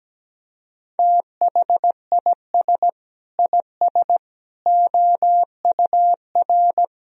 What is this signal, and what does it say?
Morse code, heard as a keyed tone, about 17 wpm, 700 Hz. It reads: THIS IS OUR
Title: Sentences from Top 1000 Words - Review Part 4 - 17wpm